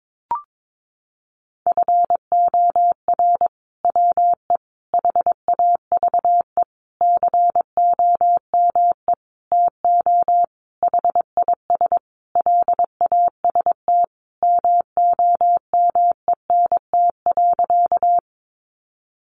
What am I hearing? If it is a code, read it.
Morse code, heard as a keyed tone, about 22 wpm, 700 Hz. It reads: FOR WE HAVE COME TO HIS LAST MOMENT.
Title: Sentences from Top 1000 Words 22wpm